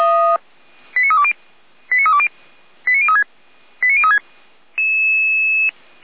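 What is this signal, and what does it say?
Five-tone selective call (ZVEI), heard as quick runs of five tones, tones of about 70 ms. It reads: 89320 89320 89036 89036
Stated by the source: ZVEI five-tone selective call received by SDR.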